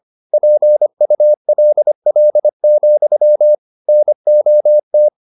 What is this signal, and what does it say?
Morse code, heard as a keyed tone, about 25 wpm, 600 Hz. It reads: PULL, NOT
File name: the old man and the sea part 1 25wpm